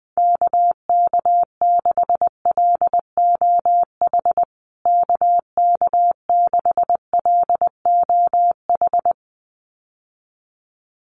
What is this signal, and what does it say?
Morse code, heard as a keyed tone, about 20 wpm, 700 Hz. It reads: KK6LOH KK6LOH
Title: USA Call Signs Repeated 2x 20wpm